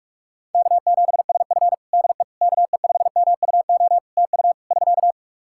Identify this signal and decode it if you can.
Morse code, heard as a keyed tone, about 46 wpm, 700 Hz. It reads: K7SF DE KE5MUO TU <SK>
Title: K1USN SST QSO Practice 45wpm